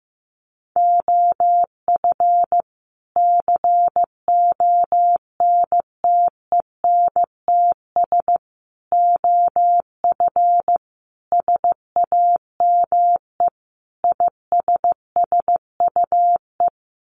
Morse code, heard as a keyed tone, about 15 wpm, 700 Hz. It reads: OF CONTENTS OF SAME ISSUE
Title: Sets of 5 Words Encore2 15wpm